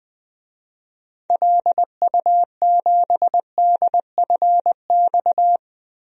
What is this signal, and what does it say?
Morse code, heard as a keyed tone, about 20 wpm, 690 Hz. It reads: LU7DFX
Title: DXCC-100 Call Signs - 6 Characters - Repeated 2x - 20wpm